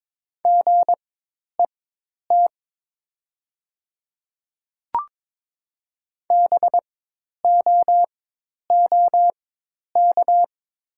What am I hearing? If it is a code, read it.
Morse code, heard as a keyed tone, about 22 wpm, spaced as for 10 wpm, 700 Hz. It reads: GET BOOK
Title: 4x Character Spacing - Top 300 Words - 22wpm